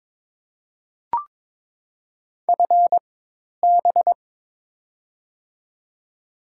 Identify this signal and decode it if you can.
Morse code, heard as a keyed tone, about 22 wpm, spaced as for 10 wpm, 700 Hz. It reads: FB
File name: Top 20 QSO Elements - 4x Character Spacing - 22wpm